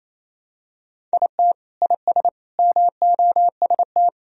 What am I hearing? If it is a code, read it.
Morse code, heard as a keyed tone, about 28 wpm, 700 Hz. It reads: IT IS MOST
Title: Sentences from Top 1000 Words - Review Part 1 - 28wpm